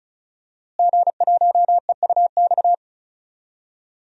Morse code, heard as a keyed tone, about 34 wpm, 700 Hz. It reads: G1EUX
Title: DXCC-100 Call Signs - 5 Characters - Repeated 2x - 35wpm